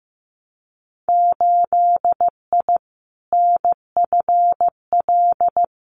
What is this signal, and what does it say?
Morse code, heard as a keyed tone, about 15 wpm, 700 Hz. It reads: 8I NFL
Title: Winter Field Day Exchanges Encore 15wpm